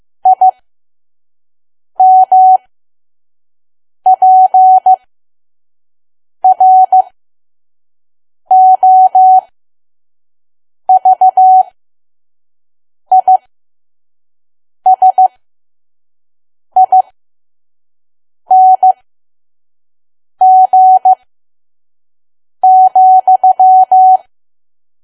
Morse code, heard as a keyed tone, about 15 wpm, spaced as for 5 wpm, 750 Hz. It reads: IMPROVISING,
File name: Archive / 5 WPM files / PM